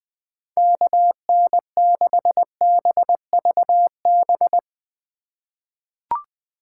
Morse code, heard as a keyed tone, about 20 wpm, 700 Hz. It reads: KN6BVB
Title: USA Call Signs - 6 Characters - 4x Character Spacing - 20wpm